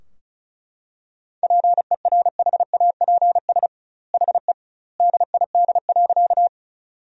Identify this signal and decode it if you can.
Morse code, heard as a keyed tone, about 34 wpm, 700 Hz. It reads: PERHAPS HE DID.
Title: 2x Word-Spacing - Sentences from Top 500 Words - 35wpm